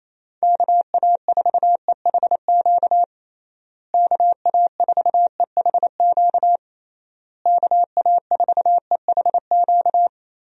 Morse code, heard as a keyed tone, about 28 wpm, 700 Hz. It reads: KA4EHQ KA4EHQ KA4EHQ
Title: USA Call Signs - 6 Characters - Repeated 3x - 28wpm